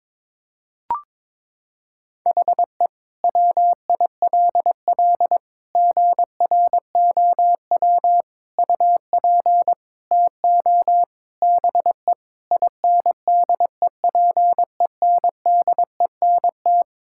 Morse code, heard as a keyed tone, about 22 wpm, 700 Hz. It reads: HE WILL GROW UP TO BE INDEPENDENT
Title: Sentences from Top 1000 Words - Review Part 6 - 22wpm